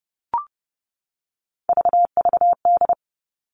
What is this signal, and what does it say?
Morse code, heard as a keyed tone, about 30 wpm, 700 Hz. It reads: VVD
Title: Three Challenging Letter-Number Combinations 30wpm